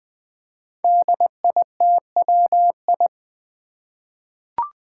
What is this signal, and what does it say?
Morse code, heard as a keyed tone, about 21 wpm, 700 Hz. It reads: DITWI
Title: Five Challenging Letter-Number Combinations 20wpm